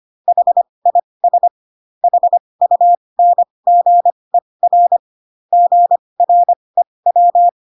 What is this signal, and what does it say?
Morse code, heard as a keyed tone, about 25 wpm, spaced as for 20 wpm, 700 Hz. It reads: HIS HUNGER GREW